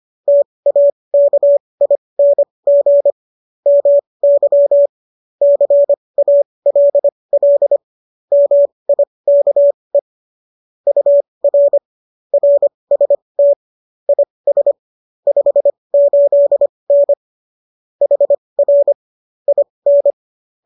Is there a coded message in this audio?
Yes: Morse code from 0.3 to 20.1 s, heard as a keyed tone, about 25 wpm, spaced as for 20 wpm, 570 Hz. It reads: TAKING MY CALL MIKE UR RST IS 58N HR IN